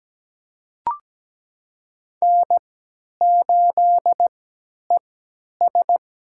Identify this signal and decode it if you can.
Morse code, heard as a keyed tone, about 17 wpm, spaced as for 10 wpm, 700 Hz. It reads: N8ES
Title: USA Call Signs - 4 Characters - 3x Character Spacing - 17wpm